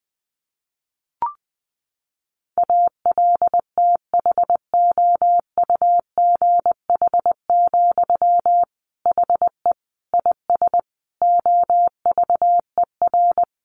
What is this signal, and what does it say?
Morse code, heard as a keyed tone, about 20 wpm, 700 Hz. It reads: ALTHOUGH, HE IS OVER
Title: Sentences from Top 600 Words 20wpm